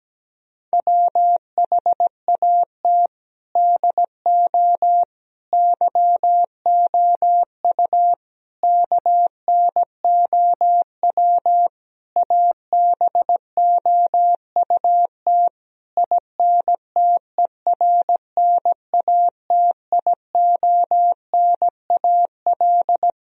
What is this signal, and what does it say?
Morse code, heard as a keyed tone, about 17 wpm, 700 Hz. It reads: WHAT DO YOU KNOW ABOUT INTERNATIONAL
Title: Sentences from Top 1000 Words - Review Part 4 - 17wpm